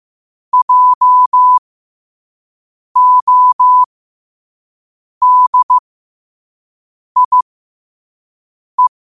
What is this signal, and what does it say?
Morse code, heard as a keyed tone, about 15 wpm, spaced as for 5 wpm, 1000 Hz. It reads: JODIE
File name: 5wpm